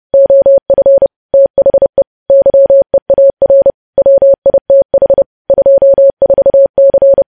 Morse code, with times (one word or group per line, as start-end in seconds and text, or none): 0.1-1.1 OF
1.3-2.0 THE
2.3-3.7 YEAR
4.0-5.2 WITH
5.5-7.2 24C